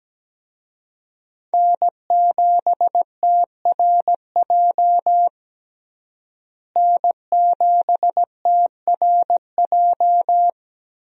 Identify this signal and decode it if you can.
Morse code, heard as a keyed tone, about 17 wpm, 700 Hz. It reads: N7TRJ N7TRJ
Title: USA Call Signs - 5 Characters - Repeated 3x - 17wpm